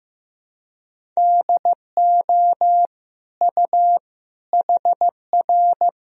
Morse code, heard as a keyed tone, about 15 wpm, 700 Hz. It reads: DO U HR
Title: Sentences from Top 300 Words and Abbreviations 15wpm